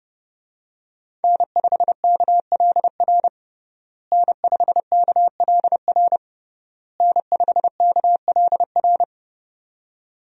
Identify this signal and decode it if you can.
Morse code, heard as a keyed tone, about 31 wpm, 700 Hz. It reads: N5KLR N5KLR N5KLR
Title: USA Call Signs - 5 Characters - Repeated 3x - 30wpm